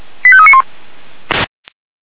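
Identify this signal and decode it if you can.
Five-tone selective call (ZVEI), heard as quick runs of five tones, tones of about 70 ms. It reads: 85391
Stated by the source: Five-tone ZVEI selective call received on an SDR.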